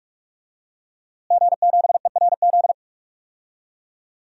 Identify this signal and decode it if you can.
Morse code, heard as a keyed tone, about 43 wpm, 700 Hz. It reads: G7ERZ
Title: DXCC-100 Call Signs - 5 Characters - Repeated 3x - 45wpm